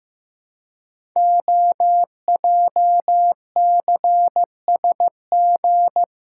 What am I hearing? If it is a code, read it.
Morse code, heard as a keyed tone, about 15 wpm, 700 Hz. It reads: OJCSG
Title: Code Groups - 5 Letters - 15wpm